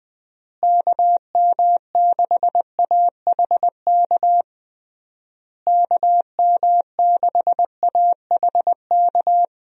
Morse code, heard as a keyed tone, about 21 wpm, 700 Hz. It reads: KM6AHK KM6AHK
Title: USA Call Signs - 6 Characters - Repeated 3x - 20wpm